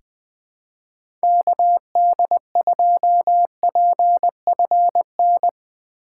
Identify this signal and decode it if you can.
Morse code, heard as a keyed tone, about 21 wpm, 700 Hz. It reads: KD2PFN
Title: USA Call Signs - 6 Characters - 8x Character Spacing - 20wpm